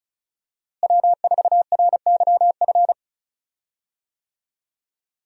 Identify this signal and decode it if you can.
Morse code, heard as a keyed tone, about 36 wpm, 700 Hz. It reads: W4RYF
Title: USA Call Signs - 5 Characters - Repeated 2x - 35wpm